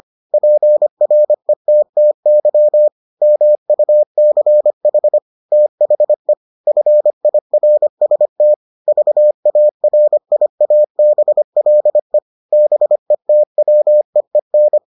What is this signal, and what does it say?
Morse code, heard as a keyed tone, about 25 wpm, 600 Hz. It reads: PRETTY MUCH THE FIRST VARIABLE BETWEEN